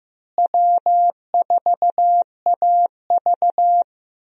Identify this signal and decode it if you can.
Morse code, heard as a keyed tone, about 15 wpm, 700 Hz. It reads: W4AV
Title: USA Call Signs - 4 Characters - Repeated 3x - 15wpm